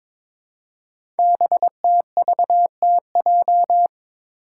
Morse code, heard as a keyed tone, about 22 wpm, 700 Hz. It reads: BTVTJ
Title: Code Groups - 5 Letters - 22wpm